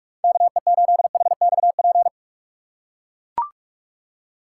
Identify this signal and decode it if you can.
Morse code, heard as a keyed tone, about 43 wpm, 700 Hz. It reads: KE8HXP
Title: USA Call Signs - 6 Characters - 2x Character Spacing - 45wpm